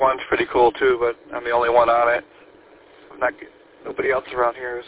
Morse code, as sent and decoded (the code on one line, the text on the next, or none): none